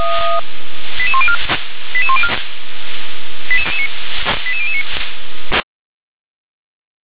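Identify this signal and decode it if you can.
Five-tone selective call (ZVEI), heard as quick runs of five tones, tones of about 70 ms. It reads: 88104 88104 88009
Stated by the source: Five-tone ZVEI selective call, address 88055, received on an SDR.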